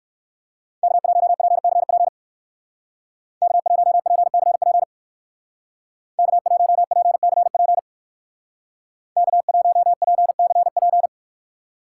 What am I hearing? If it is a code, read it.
Morse code, heard as a keyed tone, about 59 wpm, 700 Hz. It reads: K1PCP K1PCP K1PCP K1PCP